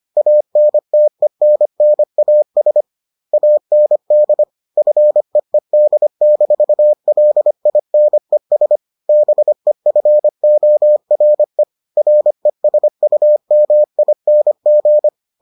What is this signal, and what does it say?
Morse code, heard as a keyed tone, about 25 wpm, 600 Hz. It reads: ANTENNAS AND FEED-LINES BEFORE RESUMING